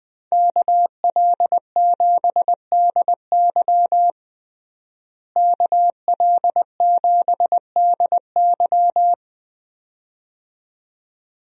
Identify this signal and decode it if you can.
Morse code, heard as a keyed tone, about 20 wpm, 700 Hz. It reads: KL7DY KL7DY
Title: DXCC-100 Call Signs - 5 Characters - Repeated 2x - 20wpm